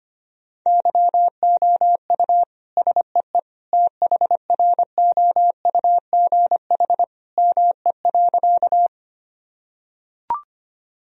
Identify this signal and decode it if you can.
Morse code, heard as a keyed tone, about 25 wpm, 700 Hz. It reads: YOU SEE THROUGH ME.
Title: Sentences from Top 1000 Words - Review Part 1 - 25wpm